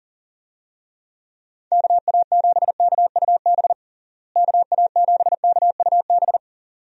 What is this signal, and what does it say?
Morse code, heard as a keyed tone, about 37 wpm, 700 Hz. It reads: KA7KUB KA7KUB
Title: USA Call Signs - 6 Characters - Repeated 2x - 40wpm